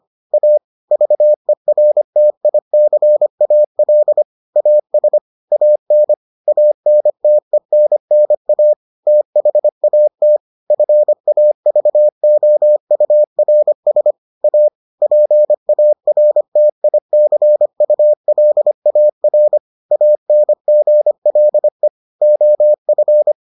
Morse code, heard as a keyed tone, about 25 wpm, 600 Hz. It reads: A VERTICAL AS AN ANTENNA THAT FAVOURS A PARTICULAR ANGLE OF